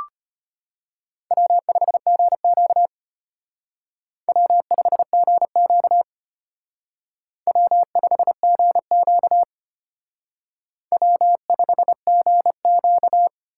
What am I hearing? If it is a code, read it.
Morse code, heard as a keyed tone, about 31 wpm, 700 Hz. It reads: W5GQ W5GQ W5GQ W5GQ
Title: Speed Racing - USA 1x2 Call Signs 25wpm